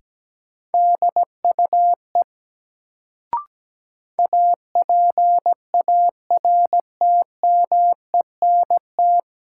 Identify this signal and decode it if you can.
Morse code, heard as a keyed tone, about 17 wpm, 700 Hz. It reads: DUE APARTMENT
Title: Sentences from Top 900 Words - New Word Drill - 17wpm